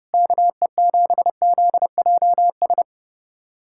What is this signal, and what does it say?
Morse code, heard as a keyed tone, about 31 wpm, 700 Hz. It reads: KE7ZJS
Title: USA Callsign DE Callsign - 6 Characters - 30wpm